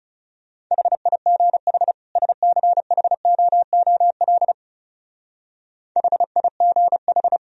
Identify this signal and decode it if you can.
Morse code, heard as a keyed tone, about 34 wpm, 700 Hz. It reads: HIGH SCHOOL HIGH
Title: Speed-Racing - Sets of 2 Words 30wpm